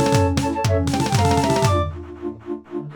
There is no code.